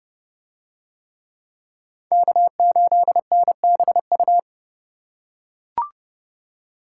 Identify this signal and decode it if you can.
Morse code, heard as a keyed tone, about 28 wpm, 700 Hz. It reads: K8NBU